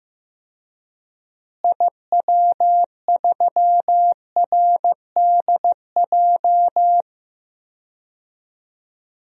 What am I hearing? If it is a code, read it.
Morse code, heard as a keyed tone, about 16 wpm, 700 Hz. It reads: IW3RDJ
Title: DXCC-100 Call Signs - 6 Characters - Repeated 2x - 15wpm